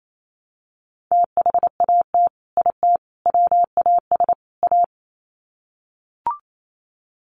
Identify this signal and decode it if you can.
Morse code, heard as a keyed tone, about 28 wpm, 700 Hz. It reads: THAT IT WAS A
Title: Sets of 4 Words 28wpm